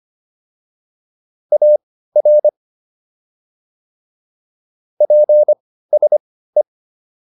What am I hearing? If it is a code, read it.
Morse code, heard as a keyed tone, about 25 wpm, spaced as for 15 wpm, 600 Hz. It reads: AR PSE